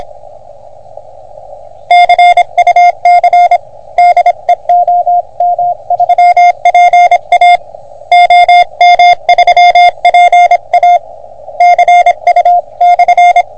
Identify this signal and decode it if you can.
Morse code, heard as a keyed tone, about 27 wpm, 720 Hz. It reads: CUC DE OM3PA OM3PA CU/
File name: OM3PA PILUP CUC 388